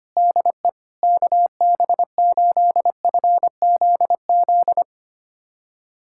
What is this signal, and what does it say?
Morse code, heard as a keyed tone, about 25 wpm, 700 Hz. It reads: DE KB8FZZ
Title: USA Callsign DE Callsign 25wpm